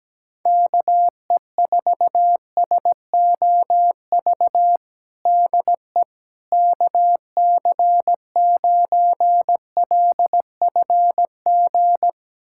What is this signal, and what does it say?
Morse code, heard as a keyed tone, about 17 wpm, 700 Hz. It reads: KE4SOV DE KC9LFG
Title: USA Callsign DE Callsign - 6 Characters - 17wpm